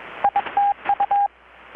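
Morse code, heard as a keyed tone, about 21 wpm, 810 Hz, with radio noise. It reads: ITU
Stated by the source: Morse from KPH's Night of Nights XX maritime radiotelegraphy event, received in Chiapas, Mexico about 3,800 km away on a Kenwood R-600 with a grounded 10 m long-wire antenna.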